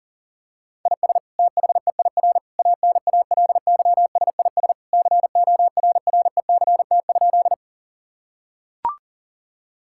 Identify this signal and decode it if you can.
Morse code, heard as a keyed tone, about 40 wpm, 700 Hz. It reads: IS THEIR ANALYSIS CORRECT?
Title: Sentences from Top 1000 Words - Review Part 5 - 40wpm